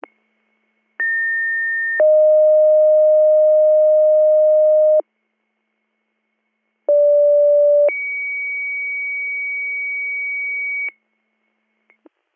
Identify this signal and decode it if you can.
Two-tone page, heard as a one-second tone, then a three-second tone, about 1800 then 620 Hz, about 580 then 2250 Hz. It is heard from 1.0 to 10.9 s.